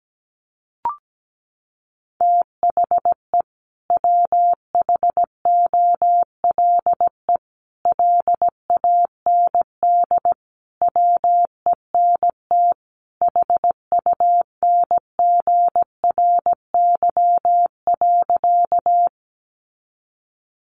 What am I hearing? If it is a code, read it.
Morse code, heard as a keyed tone, about 17 wpm, 700 Hz. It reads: THE WHOLE LAND WENT HUNGRY.
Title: Sentences from Top 1400 Words 17wpm